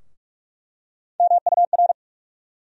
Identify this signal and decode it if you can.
Morse code, heard as a keyed tone, about 43 wpm, 700 Hz. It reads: MUR
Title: USA Call Sign Postfixes - 3 Letters 45wpm